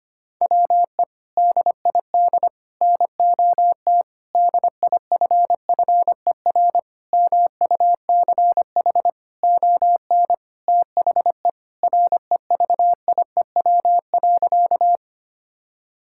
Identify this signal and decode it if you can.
Morse code, heard as a keyed tone, about 25 wpm, 700 Hz. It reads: WE DID NOT DIFFER MUCH ON THE REVIEW.
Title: Sentences from Top 1000 Words - Review Part 6 - 25wpm